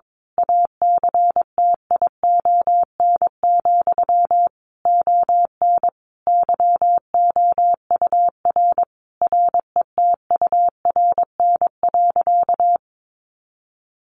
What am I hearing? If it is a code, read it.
Morse code, heard as a keyed tone, about 22 wpm, 700 Hz. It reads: ACTION, ON YOUR RETURN.
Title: A Sound Of Thunder 22wpm